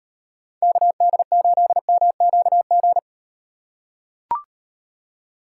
Morse code, heard as a keyed tone, about 35 wpm, 700 Hz. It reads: KD8MQG